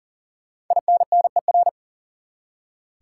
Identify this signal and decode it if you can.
Morse code, heard as a keyed tone, about 40 wpm, 700 Hz. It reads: INNER
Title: Five Letter Words - 40wpm